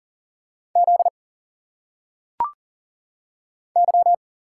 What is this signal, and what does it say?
Morse code, heard as a keyed tone, about 40 wpm, 700 Hz. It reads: Z Y
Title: Single Letters - Phonetic Pronunciation 40wpm